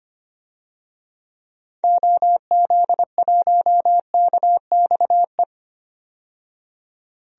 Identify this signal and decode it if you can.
Morse code, heard as a keyed tone, about 25 wpm, 700 Hz. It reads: OZ1KXE
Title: DXCC-100 Call Signs - 6 Characters - 25wpm